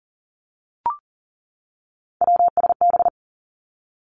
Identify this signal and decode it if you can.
Morse code, heard as a keyed tone, about 43 wpm, 700 Hz. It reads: WSB
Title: Three Challenging Letter-Number Combinations 40wpm